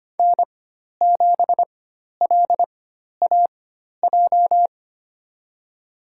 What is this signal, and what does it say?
Morse code, heard as a keyed tone, about 25 wpm, spaced as for 12 wpm, 700 Hz. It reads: N7LAJ
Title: USA Call Signs - 5 Characters - 4x Character Spacing - 25wpm